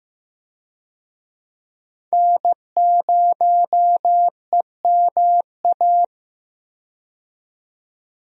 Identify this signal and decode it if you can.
Morse code, heard as a keyed tone, about 15 wpm, 700 Hz. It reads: N0EMA
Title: USA Call Signs - 5 Characters - 15wpm